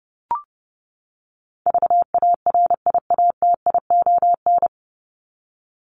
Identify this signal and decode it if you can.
Morse code, heard as a keyed tone, about 30 wpm, 700 Hz. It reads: VARIATION
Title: Top 1501-1750 Words 30wpm